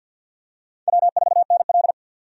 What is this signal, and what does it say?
Morse code, heard as a keyed tone, about 52 wpm, 700 Hz. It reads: W4NL
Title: USA Call Signs - 4 Characters - 50wpm